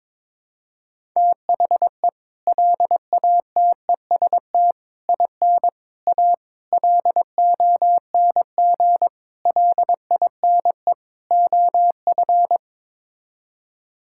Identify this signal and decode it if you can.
Morse code, heard as a keyed tone, about 22 wpm, 700 Hz. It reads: THE LATEST IN A LONG LINE OF
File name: practice / 22wpm / Sets of 7 Words - 22wpm